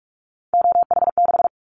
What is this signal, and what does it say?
Morse code, heard as a keyed tone, about 46 wpm, 700 Hz. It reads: OH6
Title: Callsign Prefixes for Top 25 DXCC - 45wpm